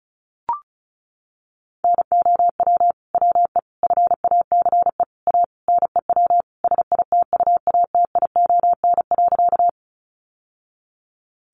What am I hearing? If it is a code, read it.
Morse code, heard as a keyed tone, about 34 wpm, 700 Hz. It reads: NOW WE FACE A NEW SITUATION.
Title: Sentences from Top 1200 Words 35wpm